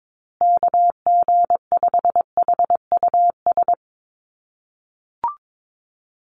Morse code, heard as a keyed tone, about 22 wpm, 700 Hz. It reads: KG5HUS
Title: USA Call Signs Repeated 3x 22wpm